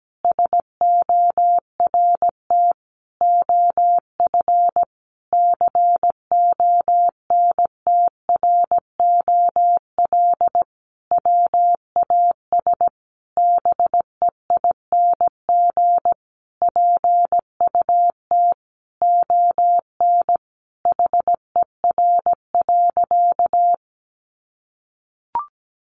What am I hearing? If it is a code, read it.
Morse code, heard as a keyed tone, about 17 wpm, 700 Hz. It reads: SORT OF CONTROL WAS BEING PUT ON HER.
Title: Sentences from Top 1300 Words 17wpm